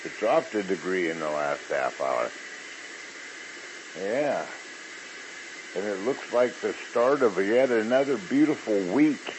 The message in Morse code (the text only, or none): none